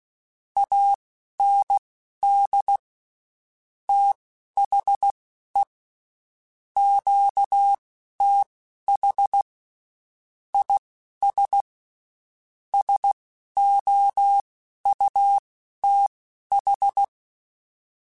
Morse code, heard as a keyed tone, about 16 wpm, spaced as for 11 wpm, 800 Hz. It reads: AND THE QTH IS SOUTH
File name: (spaced) / QSOs Spaced-mp3s / kc7sac @15 sp